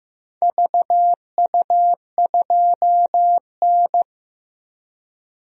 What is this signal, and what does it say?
Morse code, heard as a keyed tone, about 15 wpm, 700 Hz. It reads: VU2N
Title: DXCC-100 Call Signs - 4 Characters - Repeated 3x - 15wpm